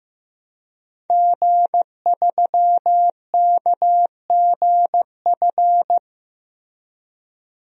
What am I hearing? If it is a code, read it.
Morse code, heard as a keyed tone, about 15 wpm, 700 Hz. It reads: G3KGF
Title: DXCC-100 Call Signs - 5 Characters - Repeated 2x - 15wpm